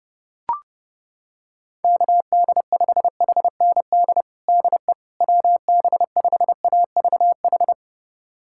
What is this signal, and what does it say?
Morse code, heard as a keyed tone, about 32 wpm, 700 Hz. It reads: KD5HND DE WB5AVH
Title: USA Callsign DE Callsign - 6 Characters - 30wpm